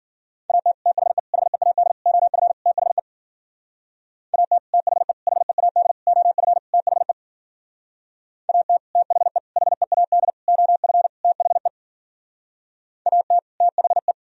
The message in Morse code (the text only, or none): AT THE HEAD OF THE AT THE HEAD OF THE AT THE HEAD OF THE AT THE